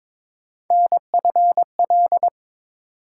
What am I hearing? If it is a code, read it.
Morse code, heard as a keyed tone, about 22 wpm, 700 Hz. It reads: NFL
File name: ARRL Sections v2 - 22wpm